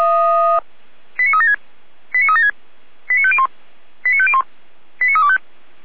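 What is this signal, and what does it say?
Five-tone selective call (ZVEI), heard as quick runs of five tones, tones of about 80 ms. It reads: 89376 89376 89501 89501 89325
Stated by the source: ZVEI five-tone selective call received on an SDR, from feuerwehr-orlamuende.org (fire brigade Orlamünde).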